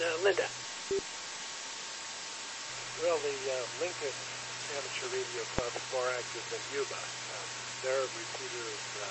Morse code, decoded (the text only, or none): none